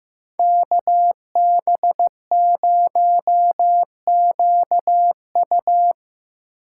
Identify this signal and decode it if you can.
Morse code, heard as a keyed tone, about 17 wpm, 700 Hz. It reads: KB0QU
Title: Speed Racing - USA Call Signs 15wpm